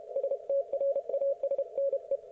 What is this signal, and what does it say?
Morse code, heard as a keyed tone, about 30 wpm, 560 Hz. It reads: STRASNE
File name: Archiv / CUC160 / ok1fhi cul160